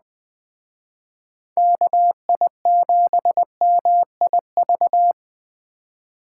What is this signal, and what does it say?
Morse code, heard as a keyed tone, about 20 wpm, 700 Hz. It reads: KI7MIV
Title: Speed Racing - USA 2x3 Call Signs 17wpm